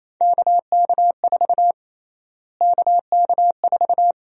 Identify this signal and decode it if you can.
Morse code, heard as a keyed tone, about 28 wpm, 700 Hz. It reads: KK4 KK4
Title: USA Call Sign Prefixes - Repeated 2x - 28wpm